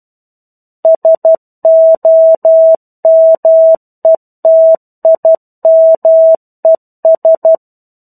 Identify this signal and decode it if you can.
Morse code, heard as a keyed tone, about 12 wpm, 650 Hz. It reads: SOMETIMES
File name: 01-Chapter-01 12 WPM